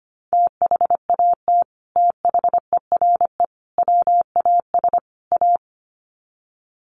Morse code, heard as a keyed tone, about 25 wpm, 700 Hz. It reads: THAT THERE WAS A
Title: Sets of 4 Words 25wpm